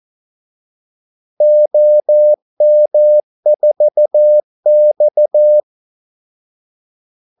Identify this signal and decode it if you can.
Morse code, heard as a keyed tone, about 14 wpm, 600 Hz. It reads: OM4X
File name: Callsigns-14wpm